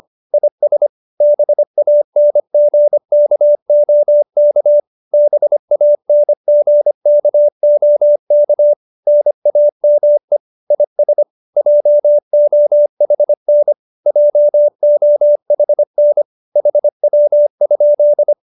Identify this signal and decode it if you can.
Morse code, heard as a keyed tone, about 25 wpm, 590 Hz. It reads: IS BANGKOK BANGKOK NAME IS JOHN JOHN HW?